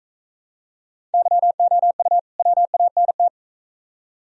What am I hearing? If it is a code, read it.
Morse code, heard as a keyed tone, about 42 wpm, 700 Hz. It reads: YOU WANT